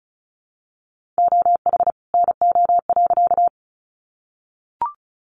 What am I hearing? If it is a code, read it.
Morse code, heard as a keyed tone, about 34 wpm, 700 Hz. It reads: OH NO.
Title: Additional Easy Phrases 35wpm